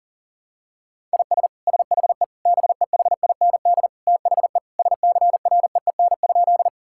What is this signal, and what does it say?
Morse code, heard as a keyed tone, about 40 wpm, 700 Hz. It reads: IS SHE BEHIND THE SCREEN?
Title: Sentences from Top 1000 Words - Review Part 6 - 40wpm